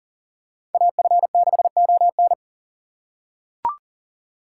Morse code, heard as a keyed tone, about 38 wpm, 700 Hz. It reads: AF6ON
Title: USA Call Signs - 5 Characters - 4x Character Spacing - 40wpm